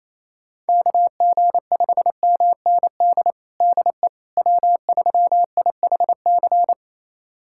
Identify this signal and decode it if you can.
Morse code, heard as a keyed tone, about 27 wpm, 700 Hz. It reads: KG5MND DE W3IHC
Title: USA Callsign DE Callsign 28wpm